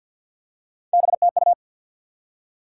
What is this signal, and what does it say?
Morse code, heard as a keyed tone, about 50 wpm, 700 Hz. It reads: BTU
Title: Top 90 QSO Elements 50wpm